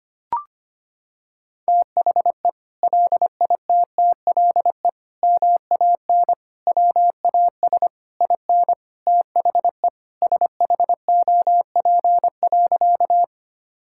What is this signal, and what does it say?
Morse code, heard as a keyed tone, about 25 wpm, 700 Hz. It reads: THE LITTLE MAN WAS IN THE SHOP.
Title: Sentences from Top 1600 Words 25wpm